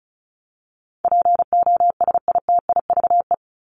Morse code, heard as a keyed tone, about 34 wpm, 700 Hz. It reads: POSITIVE
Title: Top 501-750 Words 35wpm